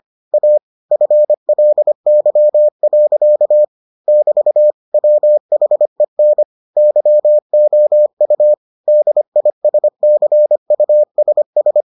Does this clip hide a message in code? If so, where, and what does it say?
Yes: Morse code from 0.3 to 11.8 s, heard as a keyed tone, about 25 wpm, 600 Hz. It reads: A FLY. <BT> WHEN YOU DISCUSS